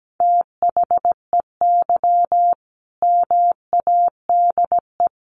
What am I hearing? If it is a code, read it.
Morse code, heard as a keyed tone, about 17 wpm, 700 Hz. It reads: THEY MADE
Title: Sentences from Top 1500 Words 17wpm